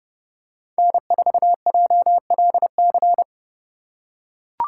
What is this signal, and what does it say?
Morse code, heard as a keyed tone, about 30 wpm, 700 Hz. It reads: N4JLC
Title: USA Call Signs - 5 Characters - 3x Character Spacing - 30wpm